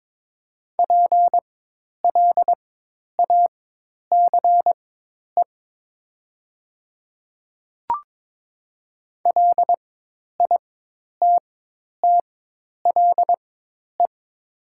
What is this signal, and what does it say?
Morse code, heard as a keyed tone, about 22 wpm, spaced as for 10 wpm, 700 Hz. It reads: PLACE LITTLE
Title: 4x Character Spacing - Top 200 Words - 22wpm